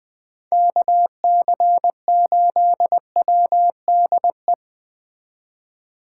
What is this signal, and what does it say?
Morse code, heard as a keyed tone, about 21 wpm, 700 Hz. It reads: KC8WDE